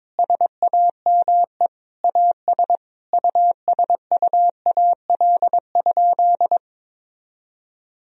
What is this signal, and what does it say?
Morse code, heard as a keyed tone, about 22 wpm, 700 Hz. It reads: SAME AS USUAL?
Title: Sentences from Top 1000 Words - Review Part 2 - 22wpm